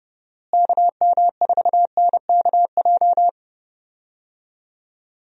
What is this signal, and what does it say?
Morse code, heard as a keyed tone, about 30 wpm, 700 Hz. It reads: KM4NKJ
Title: USA Call Signs - 6 Characters - Repeated 2x - 30wpm